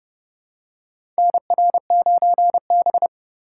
Morse code, heard as a keyed tone, about 30 wpm, 700 Hz. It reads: NR9B
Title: USA Call Signs - 4 Characters - 2x Character Spacing - 30wpm